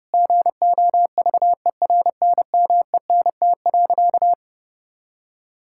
Morse code, heard as a keyed tone, about 31 wpm, 700 Hz. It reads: GOVERNMENT.
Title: Sentences from Top 1000 Words - Review Part 6 - 30wpm